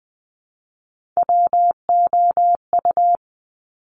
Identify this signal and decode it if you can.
Morse code, heard as a keyed tone, about 20 wpm, 700 Hz. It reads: WOU
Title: Top Three Letter Combinations in Words 20wpm